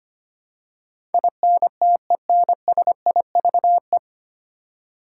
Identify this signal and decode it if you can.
Morse code, heard as a keyed tone, about 25 wpm, 700 Hz. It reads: INTENSIVE
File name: practice / 25wpm / 9 Letter Words - First 3 Letters Given - 25wpm